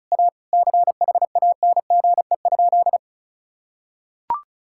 Morse code, heard as a keyed tone, about 36 wpm, 700 Hz. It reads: A CHANGE?
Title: Sentences from Top 1600 Words 35wpm